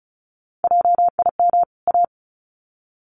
Morse code, heard as a keyed tone, about 34 wpm, 700 Hz. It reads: JIM A
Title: CWT Exchange 35wpm